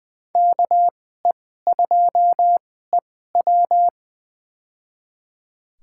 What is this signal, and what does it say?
Morse code, heard as a keyed tone, about 20 wpm, spaced as for 15 wpm, 700 Hz. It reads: KE2EW
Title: USA Call Signs - 5 Characters - 2x Character Spacing - 20wpm